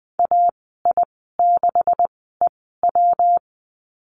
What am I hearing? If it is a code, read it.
Morse code, heard as a keyed tone, about 20 wpm, spaced as for 15 wpm, 700 Hz. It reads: AI6EW
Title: 179 Lesson 29g - Slash - Call Signs 20wpm